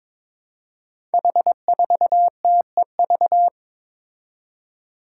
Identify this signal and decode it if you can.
Morse code, heard as a keyed tone, about 22 wpm, 700 Hz. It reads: H4TEV
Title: Five Challenging Letter-Number Combinations 22wpm